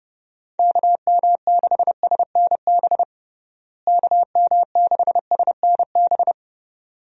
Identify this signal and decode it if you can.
Morse code, heard as a keyed tone, about 32 wpm, 700 Hz. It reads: KM6SNB KM6SNB
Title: USA Call Signs - 6 Characters - Repeated 2x - 30wpm